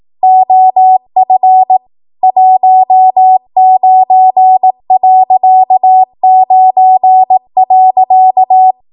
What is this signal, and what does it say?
Morse code, heard as a keyed tone, about 18 wpm, 750 Hz. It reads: OF 19.9.
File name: Archive / 18 WPM / 18WPM